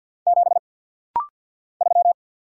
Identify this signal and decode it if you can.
Morse code, heard as a keyed tone, about 48 wpm, 700 Hz. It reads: 7 3
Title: Single Letter-Number - Rapid-Fire 50wpm